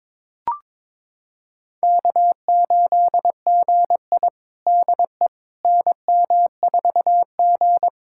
Morse code, heard as a keyed tone, about 22 wpm, 700 Hz. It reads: K8GI DE NM4G
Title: USA Callsign DE Callsign - 4 Characters - 22wpm